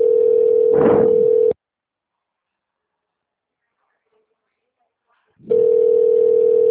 Ringback tone (North America), 440 + 480 Hz.
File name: out- -2002- - - .59